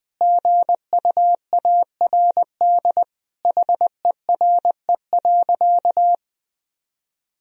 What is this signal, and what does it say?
Morse code, heard as a keyed tone, about 20 wpm, 700 Hz. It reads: GUARD HERE.
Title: Sentences from Top 1000 Words - Review Part 4 - 20wpm